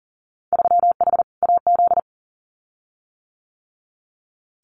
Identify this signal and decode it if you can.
Morse code, heard as a keyed tone, about 40 wpm, 700 Hz. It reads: 3H AZ
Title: Winter Field Day Exchanges Encore 40wpm